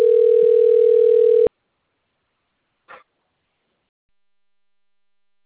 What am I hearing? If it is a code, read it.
Ringback tone (North America), 440 + 480 Hz.
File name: rg-1600-unknown- - - .1234